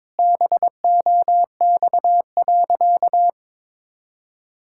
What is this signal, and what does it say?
Morse code, heard as a keyed tone, about 22 wpm, 700 Hz. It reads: BOX.
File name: practice / 22wpm / Sentences with 4-Word Phrases - 22wpm